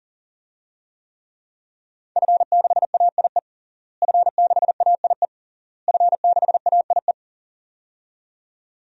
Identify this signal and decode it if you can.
Morse code, heard as a keyed tone, about 40 wpm, 700 Hz. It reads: F6AIE F6AIE F6AIE
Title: DXCC-100 Call Signs - 5 Characters - Repeated 3x - 40wpm